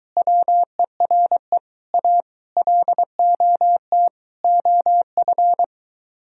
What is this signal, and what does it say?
Morse code, heard as a keyed tone, about 23 wpm, 700 Hz. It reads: WERE A LOT OF